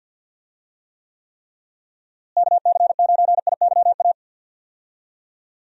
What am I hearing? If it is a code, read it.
Morse code, heard as a keyed tone, about 47 wpm, 700 Hz. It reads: KC9IYA
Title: USA Call Signs - 6 Characters - Repeated 3x - 50wpm